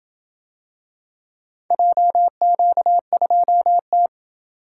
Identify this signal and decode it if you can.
Morse code, heard as a keyed tone, about 27 wpm, 700 Hz. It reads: JQ2T